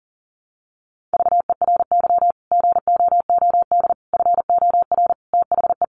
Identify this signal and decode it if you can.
Morse code, heard as a keyed tone, about 40 wpm, 700 Hz. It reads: VERY GOOD FOR THE
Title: Sentences from Top 1100 Words 40wpm